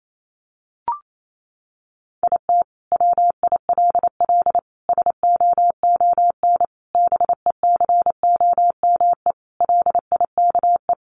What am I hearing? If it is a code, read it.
Morse code, heard as a keyed tone, about 28 wpm, 700 Hz. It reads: IT WILL SOON BECOME LIKE
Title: Sentences from Top 1000 Words 28wpm